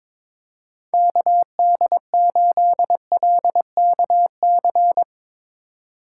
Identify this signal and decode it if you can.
Morse code, heard as a keyed tone, about 22 wpm, 700 Hz. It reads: KD8LKC
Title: USA Call Signs - 6 Characters - 2x Character Spacing - 22wpm